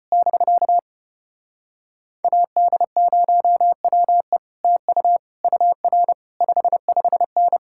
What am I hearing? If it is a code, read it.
Morse code, heard as a keyed tone, about 33 wpm, 700 Hz. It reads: <BK> AD0WE TU UR 55N